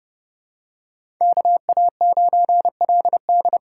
Morse code, heard as a keyed tone, about 30 wpm, 700 Hz. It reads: KA9LD
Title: USA Call Signs - 6 Characters - 8x Character Spacing - 30wpm